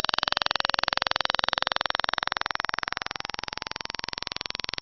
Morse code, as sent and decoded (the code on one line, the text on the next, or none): none